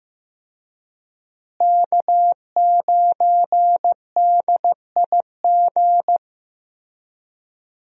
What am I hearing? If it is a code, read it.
Morse code, heard as a keyed tone, about 15 wpm, 700 Hz. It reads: K9DIG
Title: USA Call Signs - 5 Characters - Repeated 3x - 15wpm